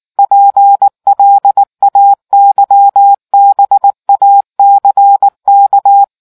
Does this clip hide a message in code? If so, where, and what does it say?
Yes: Morse code from 0.2 to 6.0 s, heard as a keyed tone, about 19 wpm, 800 Hz. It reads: PLAYBACK